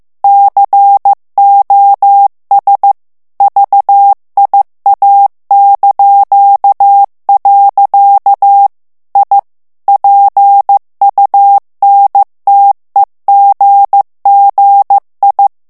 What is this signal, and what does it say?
Morse code, heard as a keyed tone, about 15 wpm, 800 Hz. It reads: COS VIA). I PUNTEGGI
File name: morse mp3 / ITALIEN texte clair 2 vitesse 14 mots minute